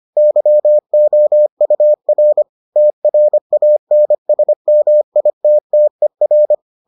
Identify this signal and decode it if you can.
Morse code, heard as a keyed tone, about 25 wpm, 600 Hz. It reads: YOUR TRANSMITTER